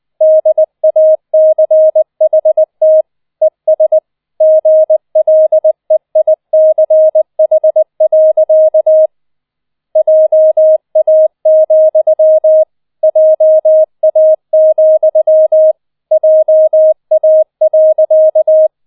Morse code, heard as a keyed tone, about 20 wpm, 630 Hz. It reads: DACHT ES GLEICH. JA, JA, JA.